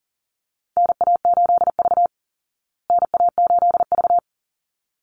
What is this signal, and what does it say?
Morse code, heard as a keyed tone, about 40 wpm, 700 Hz. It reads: NA8V NA8V
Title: Call Signs Repeated 3x - 40wpm